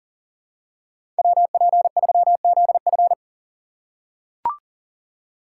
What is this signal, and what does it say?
Morse code, heard as a keyed tone, about 40 wpm, 700 Hz. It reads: WP3ZF
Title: USA Call Signs - 5 Characters - 3x Character Spacing - 40wpm